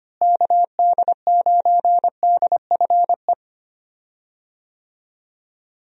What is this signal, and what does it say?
Morse code, heard as a keyed tone, about 25 wpm, 700 Hz. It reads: KD9DFE
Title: USA Call Signs - 6 Characters - Repeated 2x - 25wpm